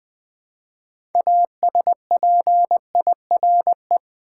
Morse code, heard as a keyed tone, about 20 wpm, 700 Hz. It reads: ASPIRE
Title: Build A Word - 2 Letters At A Time - 6-Letter Words - 20wpm